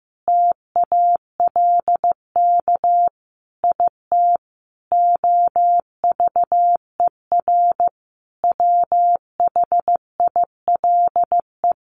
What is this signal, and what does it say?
Morse code, heard as a keyed tone, about 15 wpm, 700 Hz. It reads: TALK IT OVER WHILE
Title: Sentences from Top 1300 Words 15wpm